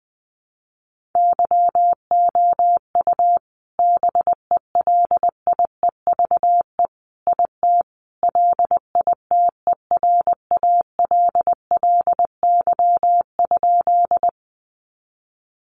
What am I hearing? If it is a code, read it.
Morse code, heard as a keyed tone, about 20 wpm, 700 Hz. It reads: YOU BELIEVE IT LITERALLY?
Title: Sentences from Top 1400 Words 20wpm